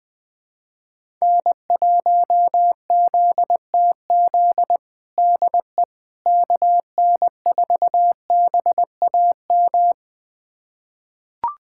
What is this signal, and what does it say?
Morse code, heard as a keyed tone, about 20 wpm, 700 Hz. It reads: N1ZTZ DE KN4BAM
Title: USA Callsign DE Callsign 20wpm